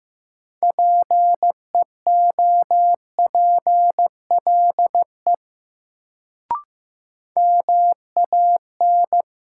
Morse code, heard as a keyed tone, about 15 wpm, 700 Hz. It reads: PEOPLE MAN
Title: Sentences from Top 200 Words - New Word Drill - 15wpm